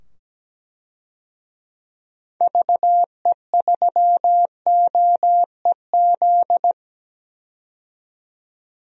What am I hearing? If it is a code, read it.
Morse code, heard as a keyed tone, about 17 wpm, 700 Hz. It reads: VE3OEZ
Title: DXCC-100 Call Signs - 6 Characters - 17wpm